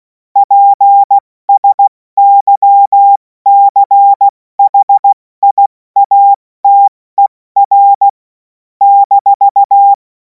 Morse code, heard as a keyed tone, about 16 wpm, 800 Hz. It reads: PSYCHIATER -
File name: Morsekurs / Witze / Witz 26. Mai 2026 70 F80